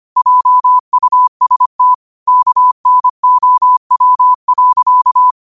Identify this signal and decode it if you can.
Morse code, heard as a keyed tone, about 25 wpm, 1000 Hz. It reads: JUST KNOW.